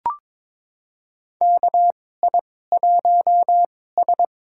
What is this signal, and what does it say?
Morse code, heard as a keyed tone, about 22 wpm, spaced as for 16 wpm, 700 Hz. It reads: KI1S
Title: USA Call Signs - 4 Characters - 2x Character Spacing - 22wpm